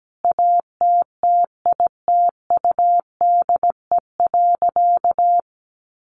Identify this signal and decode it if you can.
Morse code, heard as a keyed tone, about 17 wpm, 700 Hz. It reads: ATTITUDE.
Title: Sentences from Top 1300 Words 17wpm